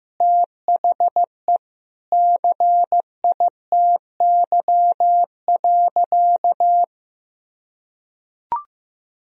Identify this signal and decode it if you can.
Morse code, heard as a keyed tone, about 16 wpm, 700 Hz. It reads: THE CITY.
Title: Sentences from Top 1000 Words - Review Part 4 - 15wpm